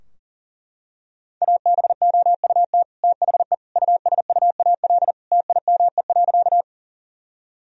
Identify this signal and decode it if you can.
Morse code, heard as a keyed tone, about 37 wpm, 700 Hz. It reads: ABOUT THE USUAL TIME.
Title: Sentences from Top 1000 Words - Review Part 2 - 40wpm